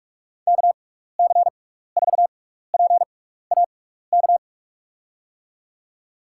Morse code, heard as a keyed tone, about 44 wpm, spaced as for 15 wpm, 700 Hz. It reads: KC4PAK
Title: USA Call Signs - 6 Characters - 6x Character Spacing - 45wpm